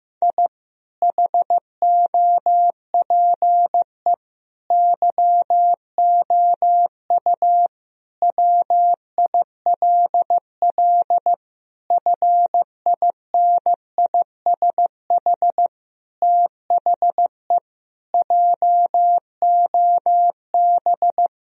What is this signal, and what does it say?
Morse code, heard as a keyed tone, about 15 wpm, 700 Hz. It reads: I HOPE YOU WILL FINISH THE JOB
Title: Sentences from Top 1000 Words - Review Part 3 - 15wpm